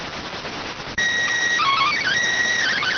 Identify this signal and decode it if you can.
SSTV (slow-scan television), PD 290.